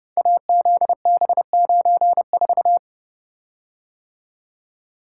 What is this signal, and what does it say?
Morse code, heard as a keyed tone, about 30 wpm, 700 Hz. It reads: AZB94
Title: Code Groups - 5 Letter:Numbers - 30wpm